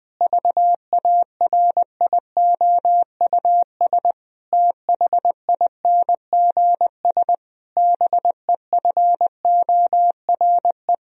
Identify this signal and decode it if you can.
Morse code, heard as a keyed tone, about 20 wpm, 700 Hz. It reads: VARIOUS THINGS BEFORE
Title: Sentences from Top 1000 Words - Review Part 4 - 20wpm